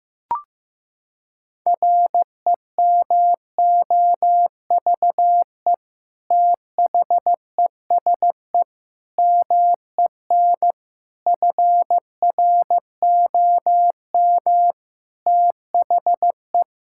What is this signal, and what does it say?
Morse code, heard as a keyed tone, about 15 wpm, 700 Hz. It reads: REMOVE THESE MEN FROM THE
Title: Sentences from Top 1000 Words - Review Part 5 - 15wpm